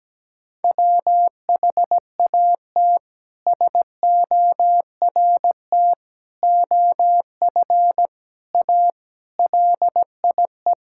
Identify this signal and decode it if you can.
Morse code, heard as a keyed tone, about 17 wpm, 700 Hz. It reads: WHAT SORT OF A LIE
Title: Sentences from Top 1600 Words 17wpm